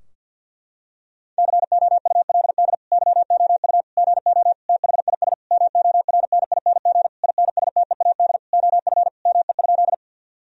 Morse code, heard as a keyed tone, about 50 wpm, 700 Hz. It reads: COULD YOU GO THIS MORNING INSTEAD OF ME?
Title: Sentences from Top 1000 Words - Review Part 4 - 50wpm